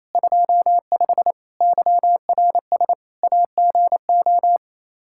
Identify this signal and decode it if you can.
Morse code, heard as a keyed tone, about 28 wpm, 700 Hz. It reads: 25 YRS AGO
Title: Practice QSO - Exchange 3 - More Info - 28wpm